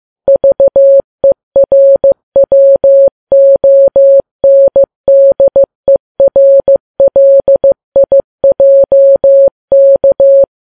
Morse code, heard as a keyed tone, about 15 wpm, 550 Hz. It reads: VERWONDERLIJK
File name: Files Web tls / Morse on9fv / morse15kps